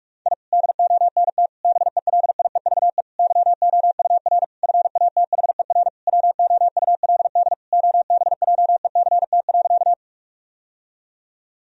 Morse code, heard as a keyed tone, about 48 wpm, 700 Hz. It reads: I DONT BELIEVE YOUR FATHER WOULD OBJECT.
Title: Sentences from Top 1000 Words - Review Part 3 - 45wpm